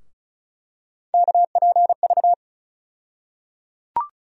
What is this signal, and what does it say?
Morse code, heard as a keyed tone, about 34 wpm, 700 Hz. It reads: KPV
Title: USA Call Sign Postfixes - 3 Letters 35wpm